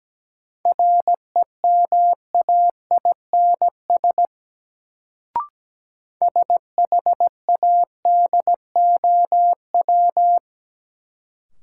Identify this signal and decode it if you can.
Morse code, heard as a keyed tone, about 17 wpm, 700 Hz. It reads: REMAINS SHADOW
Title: Sets of 5 Words Encore - New Word Drill - 17wpm